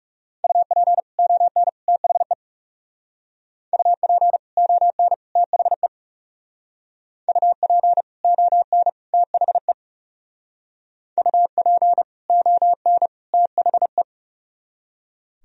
Morse code, heard as a keyed tone, about 36 wpm, 700 Hz. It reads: UP ON THE UP ON THE UP ON THE UP ON THE
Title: Speed-Racing - Sets of 3 Words 30wpm